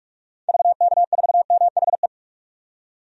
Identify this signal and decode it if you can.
Morse code, heard as a keyed tone, about 42 wpm, 700 Hz. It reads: VK4MHE
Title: DXCC-100 Call Signs - 6 Characters - 45wpm